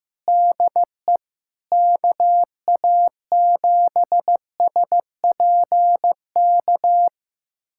Morse code, heard as a keyed tone, about 15 wpm, 700 Hz. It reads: DE KA7SPK
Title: USA Callsign DE Callsign 15wpm